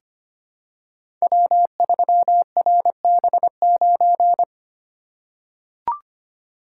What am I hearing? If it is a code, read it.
Morse code, heard as a keyed tone, about 25 wpm, 700 Hz. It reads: W3RB9
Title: Code Groups - 5 Letter:Numbers - 25wpm